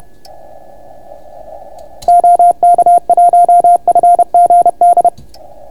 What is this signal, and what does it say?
Morse code, heard as a keyed tone, about 30 wpm, 690 Hz. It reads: OK1FGD